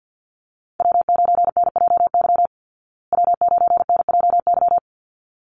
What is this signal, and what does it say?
Morse code, heard as a keyed tone, about 52 wpm, 700 Hz. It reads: W9NJY W9NJY
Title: Call Signs Repeated 3x - 50wpm